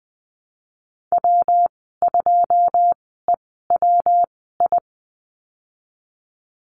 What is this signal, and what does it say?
Morse code, heard as a keyed tone, about 20 wpm, spaced as for 14 wpm, 700 Hz. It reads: W2EWI
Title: 173 Lesson 28g - Number 9 - Call Signs 20wpm